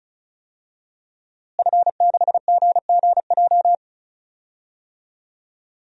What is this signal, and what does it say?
Morse code, heard as a keyed tone, about 34 wpm, 700 Hz. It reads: F6GGJ
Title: DXCC-100 Call Signs - 5 Characters - Repeated 2x - 35wpm